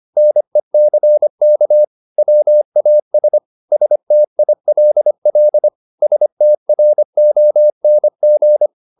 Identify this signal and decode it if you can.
Morse code, heard as a keyed tone, about 25 wpm, 600 Hz. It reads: NECK WAS STILL STRONG
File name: the old man and the sea part 1 25wpm